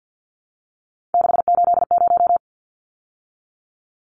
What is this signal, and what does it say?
Morse code, heard as a keyed tone, about 50 wpm, 700 Hz. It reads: 680
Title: Numbers - 3-Digit 50wpm